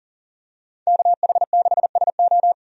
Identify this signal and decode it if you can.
Morse code, heard as a keyed tone, about 38 wpm, 700 Hz. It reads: KH6SO
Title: DXCC-100 Call Signs - 5 Characters - 40wpm